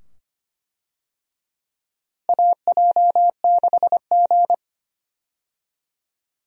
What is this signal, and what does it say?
Morse code, heard as a keyed tone, about 25 wpm, 700 Hz. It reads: AJ6G
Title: USA Call Signs - 4 Characters - Repeated 2x - 25wpm